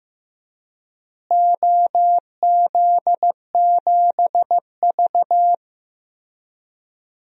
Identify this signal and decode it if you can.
Morse code, heard as a keyed tone, about 15 wpm, 700 Hz. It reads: OZ7V
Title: DXCC-100 Call Signs - 4 Characters - Repeated 3x - 15wpm